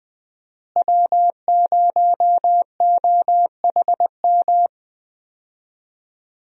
Morse code, heard as a keyed tone, about 20 wpm, 700 Hz. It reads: W0OHM
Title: USA Call Signs - 5 Characters - Repeated 2x - 20wpm